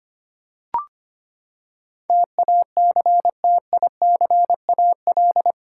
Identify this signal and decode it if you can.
Morse code, heard as a keyed tone, about 25 wpm, 700 Hz. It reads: TACTICAL
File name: Eight Letter Words - 25wpm